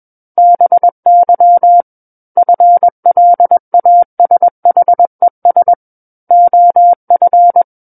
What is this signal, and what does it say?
Morse code, heard as a keyed tone, about 21 wpm, 700 Hz. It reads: BY FLASHES OF
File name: Book One / 12-Chapter 12 - 21 WPM